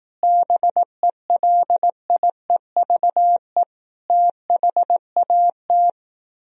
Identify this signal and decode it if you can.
Morse code, heard as a keyed tone, about 18 wpm, 700 Hz. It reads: BELIEVE THAT
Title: Speed-Racing - Sets of 5 Words 15wpm